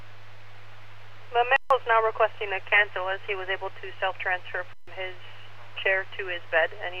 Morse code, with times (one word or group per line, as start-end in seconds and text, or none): none